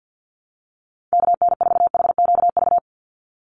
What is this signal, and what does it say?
Morse code, heard as a keyed tone, about 50 wpm, 700 Hz. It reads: KN4HQV
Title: USA Call Signs 50wpm